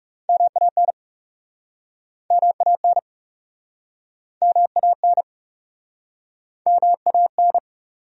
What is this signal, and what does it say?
Morse code, heard as a keyed tone, about 42 wpm, 700 Hz. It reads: MAN MAN MAN MAN